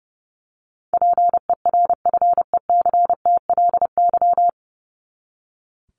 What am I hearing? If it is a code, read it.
Morse code, heard as a keyed tone, about 30 wpm, 700 Hz. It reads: PERFECTLY
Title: Top 1501-1750 Words 30wpm